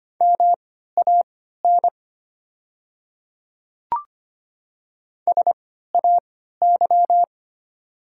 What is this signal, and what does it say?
Morse code, heard as a keyed tone, about 25 wpm, spaced as for 14 wpm, 700 Hz. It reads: MAN SAY
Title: 3x Character Spacing - Top 200 Words - 25wpm